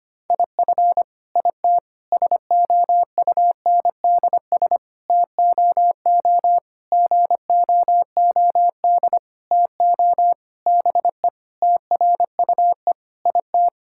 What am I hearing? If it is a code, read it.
Morse code, heard as a keyed tone, about 25 wpm, 700 Hz. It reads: IF IT SOUNDS TOO GOOD TO BE TRUE IT
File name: Sets of 9 Words - 25wpm